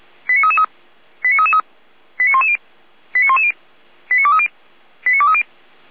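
Five-tone selective call (ZVEI), heard as quick runs of five tones, tones of about 70 ms. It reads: 89333 89333 89110 89110 89230 89230
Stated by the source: ZVEI five-tone call received on an SDR.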